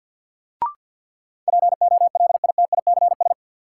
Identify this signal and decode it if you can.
Morse code, heard as a keyed tone, about 52 wpm, 700 Hz. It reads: POLITICS
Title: Sentences from Top 1000 Words - New Word Drill - 50wpm